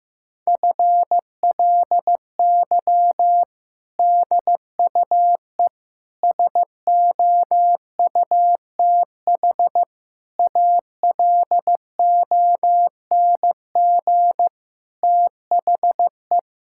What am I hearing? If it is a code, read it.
Morse code, heard as a keyed tone, about 15 wpm, 700 Hz. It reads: FLY DUE SOUTH ALONG THE
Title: Sentences from Top 1000 Words - Review Part 6 - 15wpm